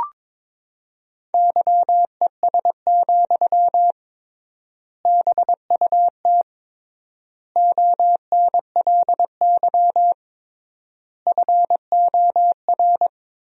Morse code, heard as a keyed tone, about 22 wpm, 700 Hz. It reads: YES, BUT ONLY FOR
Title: 3x Word-Spacing - Sentences from Top 500 Words - 22wpm